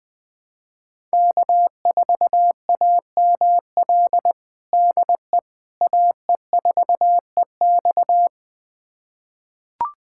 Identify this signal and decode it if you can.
Morse code, heard as a keyed tone, about 20 wpm, 700 Hz. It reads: K4AML DE AE4EX
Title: USA Callsign DE Callsign - 5 Characters - 20wpm